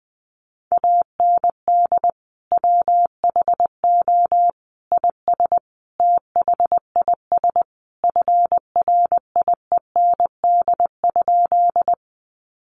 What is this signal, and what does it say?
Morse code, heard as a keyed tone, about 20 wpm, 700 Hz. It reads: AND WHO IS THIS FRIEND?
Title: Sentences from Top 300 Words 20wpm